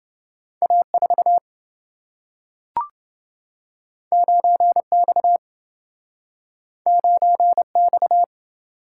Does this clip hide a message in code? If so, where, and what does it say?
Yes: Morse code from 0.6 to 8.2 s, heard as a keyed tone, about 27 wpm, 700 Hz. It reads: A4 9X 9X